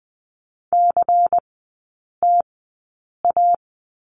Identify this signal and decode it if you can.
Morse code, heard as a keyed tone, about 20 wpm, spaced as for 8 wpm, 700 Hz. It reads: CTA
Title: 100 Lesson 17e - Letter u - Three Characters 20wpm